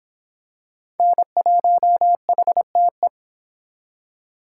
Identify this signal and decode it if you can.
Morse code, heard as a keyed tone, about 26 wpm, 700 Hz. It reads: N1HTE